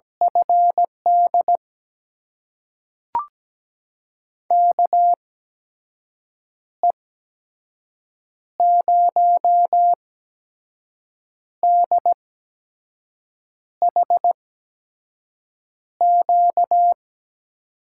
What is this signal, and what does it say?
Morse code, heard as a keyed tone, about 17 wpm, spaced as for 5 wpm, 700 Hz. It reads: FD KE0DHQ
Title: USA Call Signs - 6 Characters - 8x Character Spacing - 17wpm